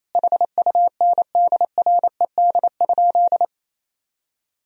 Morse code, heard as a keyed tone, about 28 wpm, 700 Hz. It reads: HUNDRED?
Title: Sentences from Top 400 Words and Abbreviations 28wpm